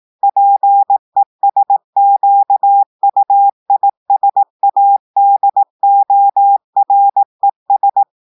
Morse code, heard as a keyed tone, about 18 wpm, 800 Hz. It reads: PESQUISADORES
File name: audios / portugais / portugais texte en clair 1 vitesse 18 mots minutes